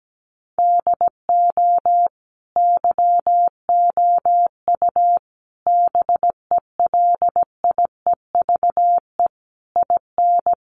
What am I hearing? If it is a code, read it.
Morse code, heard as a keyed tone, about 17 wpm, 700 Hz. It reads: DO YOU BELIEVE IN
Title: Sentences from Top 1200 Words 17wpm